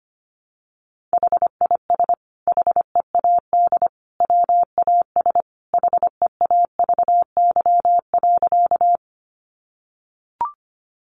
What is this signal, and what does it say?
Morse code, heard as a keyed tone, about 25 wpm, 700 Hz. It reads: HIS HEAD WAS HEAVY.
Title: Sentences from Top 1100 Words 25wpm